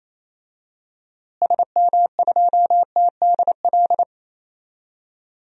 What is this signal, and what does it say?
Morse code, heard as a keyed tone, about 28 wpm, 700 Hz. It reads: SM2TDL
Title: DXCC-100 Call Signs - 6 Characters - Repeated 3x - 28wpm